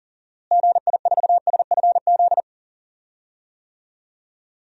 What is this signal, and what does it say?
Morse code, heard as a keyed tone, about 37 wpm, 700 Hz. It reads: GI4SFZ
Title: DXCC-100 Call Signs - 6 Characters - Repeated 2x - 40wpm